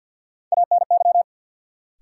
Morse code, heard as a keyed tone, about 53 wpm, 700 Hz. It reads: ANY
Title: Sentences from Top 200 Words - New Word Drill - 50wpm